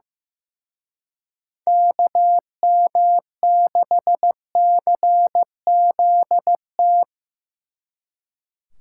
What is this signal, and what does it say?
Morse code, heard as a keyed tone, about 18 wpm, 700 Hz. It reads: KM6CZT
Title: Speed Racing - USA 2x3 Call Signs 15wpm